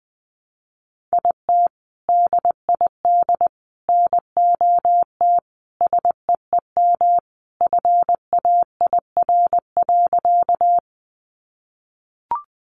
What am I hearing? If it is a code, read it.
Morse code, heard as a keyed tone, about 20 wpm, 700 Hz. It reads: IT DID NOT SEEM FAIR.
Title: Sentences from Top 1300 Words 20wpm